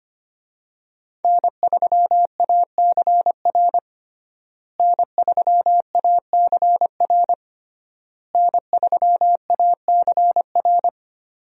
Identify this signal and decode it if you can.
Morse code, heard as a keyed tone, about 25 wpm, 700 Hz. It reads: N3ACR N3ACR N3ACR
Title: USA Call Signs - 5 Characters - Repeated 3x - 25wpm